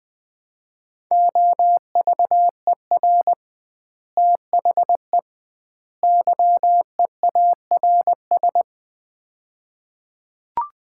Morse code, heard as a keyed tone, about 21 wpm, 700 Hz. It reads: OVER THE YEARS
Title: 2x Word-Spacing - Sets of 3 Words - 20wpm